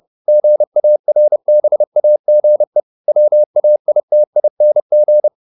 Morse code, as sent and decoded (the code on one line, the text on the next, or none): --. .- .-. -... .- --. / .-- .- .. - .. -. --.
GARBAGE WAITING